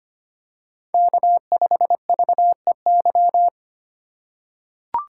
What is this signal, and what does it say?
Morse code, heard as a keyed tone, about 25 wpm, 700 Hz. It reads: K5VEY
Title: USA Call Signs - 5 Characters - 3x Character Spacing - 25wpm